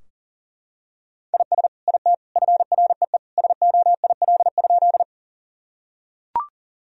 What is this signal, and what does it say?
Morse code, heard as a keyed tone, about 40 wpm, 700 Hz. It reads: IS IT FREE SOIL?
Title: Sentences from Top 1000 Words - Review Part 4 - 40wpm